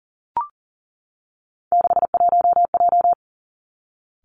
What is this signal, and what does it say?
Morse code, heard as a keyed tone, about 42 wpm, 700 Hz. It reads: 61J
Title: Three Challenging Letter-Number Combinations 40wpm